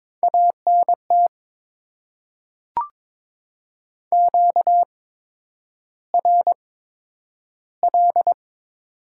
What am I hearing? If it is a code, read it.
Morse code, heard as a keyed tone, about 22 wpm, spaced as for 9 wpm, 700 Hz. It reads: ANT QRL
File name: Top 20 QSO Elements - 8x Character Spacing - 22wpm